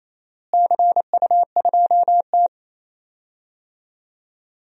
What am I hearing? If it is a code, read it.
Morse code, heard as a keyed tone, about 28 wpm, 700 Hz. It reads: CU2T
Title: DXCC-100 Call Signs - 4 Characters - Repeated 3x - 28wpm